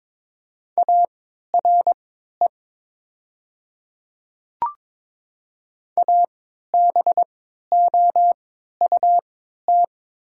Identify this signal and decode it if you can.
Morse code, heard as a keyed tone, about 22 wpm, spaced as for 12 wpm, 700 Hz. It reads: ARE ABOUT
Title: 3x Character Spacing - Top 200 Words - 22wpm